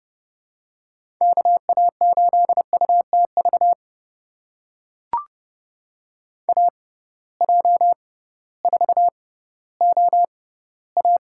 Morse code, heard as a keyed tone, about 32 wpm, spaced as for 17 wpm, 700 Hz. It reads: KA8UTV AJ4OA
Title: USA Call Signs - 6x Character Spacing - 30wpm